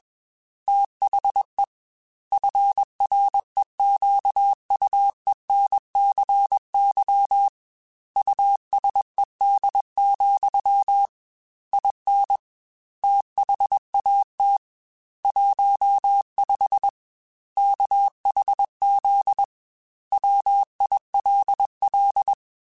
Morse code, heard as a keyed tone, about 21 wpm, 800 Hz. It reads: THE FREQUENCY USED, IN THAT 15 KHZ WILL
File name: Technical Subjects / Technical Subjects-mp3s / Sw Pwr Sup @20